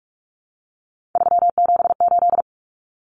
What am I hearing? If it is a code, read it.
Morse code, heard as a keyed tone, about 44 wpm, 700 Hz. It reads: 378
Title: Common Cut Numbers - 3-Digit 45wpm